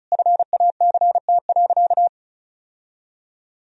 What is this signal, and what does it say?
Morse code, heard as a keyed tone, about 36 wpm, 700 Hz. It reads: FACT.
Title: Sentences from Top 1000 Words - Review Part 4 - 35wpm